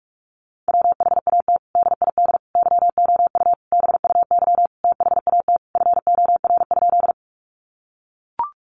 Morse code, heard as a keyed tone, about 44 wpm, 700 Hz. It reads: WHAT DID YOU BUY THAT FOR?
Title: Sentences from Top 700 Words 45wpm